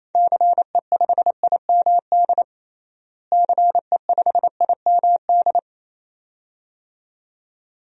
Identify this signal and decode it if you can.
Morse code, heard as a keyed tone, about 28 wpm, 700 Hz. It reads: CE5IMD CE5IMD
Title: DXCC-100 Call Signs - 6 Characters - Repeated 2x - 28wpm